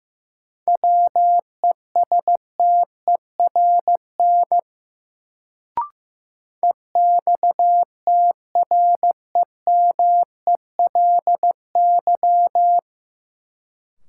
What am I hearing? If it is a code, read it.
Morse code, heard as a keyed tone, about 15 wpm, 700 Hz. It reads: WESTERN EXTREMELY
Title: Sentences from Top 1000 Words - Review Part 7 - New Word Drill - 15wpm